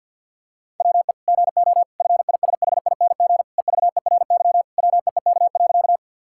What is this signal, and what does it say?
Morse code, heard as a keyed tone, about 50 wpm, 700 Hz. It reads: WE GO FISHING EVERY WEEK.
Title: Sentences from Top 1000 Words - Review Part 5 - 50wpm